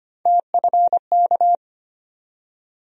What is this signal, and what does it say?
Morse code, heard as a keyed tone, about 25 wpm, 700 Hz. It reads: TFK
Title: USA Call Sign Postfixes - 3 Letters 25wpm